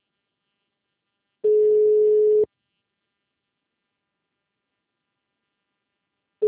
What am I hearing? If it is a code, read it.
Ringback tone (Europe, 425 Hz).